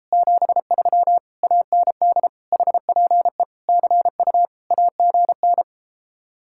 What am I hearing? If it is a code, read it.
Morse code, heard as a keyed tone, about 33 wpm, 700 Hz. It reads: 73 AND HPE CU AGN